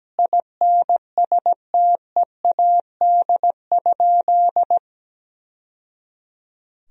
Morse code, heard as a keyed tone, about 17 wpm, 700 Hz. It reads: INSTEAD?
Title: Sentences from Top 1000 Words - Review Part 4 - 17wpm